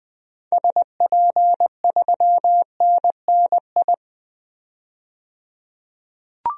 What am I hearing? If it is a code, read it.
Morse code, heard as a keyed tone, about 21 wpm, 700 Hz. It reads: SP3NNI
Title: DXCC-100 Call Signs - 6 Characters - 20wpm